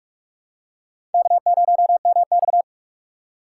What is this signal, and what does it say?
Morse code, heard as a keyed tone, about 46 wpm, 700 Hz. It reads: K0MX
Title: USA Call Signs - 4 Characters - 3x Character Spacing - 45wpm